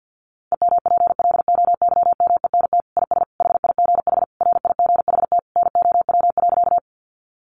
Morse code, heard as a keyed tone, about 50 wpm, 700 Hz. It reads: EMPLOYMENT IS HIGH RIGHT NOW.
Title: Sentences from Top 1500 Words 50wpm